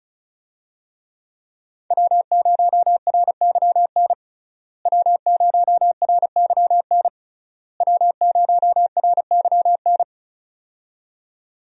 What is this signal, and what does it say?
Morse code, heard as a keyed tone, about 36 wpm, 700 Hz. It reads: W0RYN W0RYN W0RYN
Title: USA Call Signs - 5 Characters - Repeated 3x - 35wpm